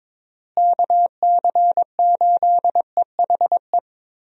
Morse code, heard as a keyed tone, about 22 wpm, 700 Hz. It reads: KC8EHE
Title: USA Call Signs - 6 Characters - Repeated 3x - 22wpm